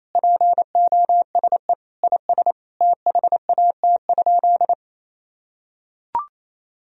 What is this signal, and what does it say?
Morse code, heard as a keyed tone, about 28 wpm, 700 Hz. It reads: POSE IS THAT?
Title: Sentences from Top 1000 Words - Review Part 2 - 28wpm